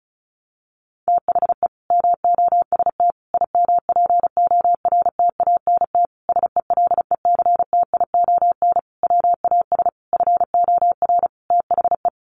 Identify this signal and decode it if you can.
Morse code, heard as a keyed tone, about 34 wpm, 700 Hz. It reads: THE MOST IMPORTANT SELECTION WAS FOR THE
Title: Sentences from Top 1400 Words 35wpm